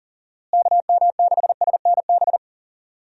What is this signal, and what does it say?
Morse code, heard as a keyed tone, about 39 wpm, 700 Hz. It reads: KM6SNB
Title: USA Call Signs - 6 Characters - Repeated 2x - 40wpm